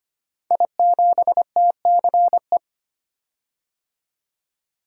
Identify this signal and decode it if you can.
Morse code, heard as a keyed tone, about 25 wpm, 700 Hz. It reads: I7TCE
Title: Code Groups - 5 Letter:Numbers - 25wpm